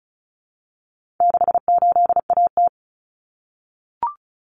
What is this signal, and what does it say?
Morse code, heard as a keyed tone, about 34 wpm, 700 Hz. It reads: 68AT
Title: Common Cut Numbers - 4-Digit 35wpm